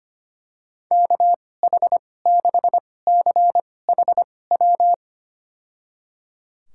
Morse code, heard as a keyed tone, about 25 wpm, spaced as for 18 wpm, 700 Hz. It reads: KH6CHW
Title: USA Call Signs - 6 Characters - 2x Character Spacing - 25wpm